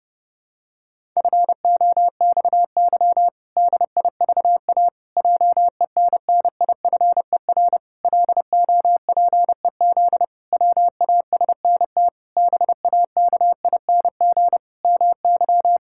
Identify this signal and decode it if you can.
Morse code, heard as a keyed tone, about 30 wpm, 700 Hz. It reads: FOXY DIVA JENNIFER LOPEZ WASNT BAKING MY
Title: Pangram Challenge v2 - 30wpm